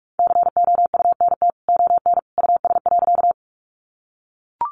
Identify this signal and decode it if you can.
Morse code, heard as a keyed tone, about 44 wpm, 700 Hz. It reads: COUNT ON US.
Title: Sentences from Top 1500 Words 45wpm